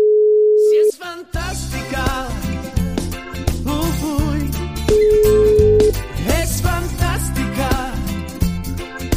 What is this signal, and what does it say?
Ringback tone (Europe, 425 Hz), with noise.